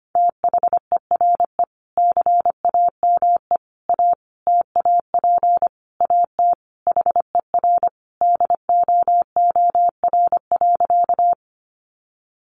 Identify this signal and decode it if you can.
Morse code, heard as a keyed tone, about 25 wpm, 700 Hz. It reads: THERE CAME A TAP AT HER DOOR.
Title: Sentences from Top 1400 Words 25wpm